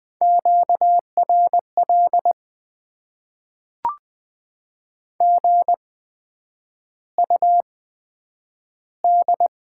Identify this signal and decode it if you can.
Morse code, heard as a keyed tone, about 20 wpm, spaced as for 9 wpm, 700 Hz. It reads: QRL GUD
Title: Top 20 QSO Elements - 8x Character Spacing - 20wpm